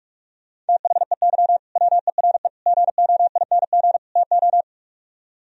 Morse code, heard as a keyed tone, about 60 wpm, spaced as for 52 wpm, 700 Hz. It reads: THEY WERE GOING TO
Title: Speed-Racing - Sets of 4 Words 45wpm